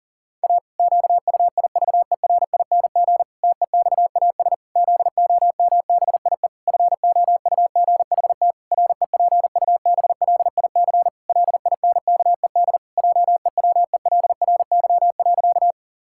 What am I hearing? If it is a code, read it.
Morse code, heard as a keyed tone, about 40 wpm, 700 Hz. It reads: A QUIVERING TEXAS ZOMBIE FOUGHT REPUBLIC LINKED JEWELRY.
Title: Pangram Challenge v2 - 40wpm